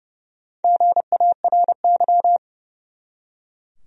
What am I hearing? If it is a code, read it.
Morse code, heard as a keyed tone, about 31 wpm, 700 Hz. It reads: GARY
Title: USA Common Names - Max 5 Letters - 30wpm